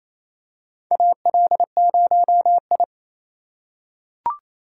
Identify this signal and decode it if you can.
Morse code, heard as a keyed tone, about 28 wpm, 700 Hz. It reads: AL0I
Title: USA Call Signs - 4 Characters - 3x Character Spacing - 28wpm